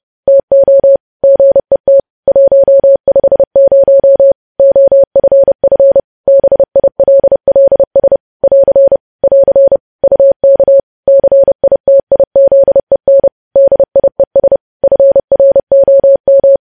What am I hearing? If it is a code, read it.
Morse code, heard as a keyed tone, about 30 wpm, 550 Hz. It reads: TO GET 150 OFF BILLS <AR> <AR> UK CITIZEN DIES FROM
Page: news headlines in morse code at 30 wpm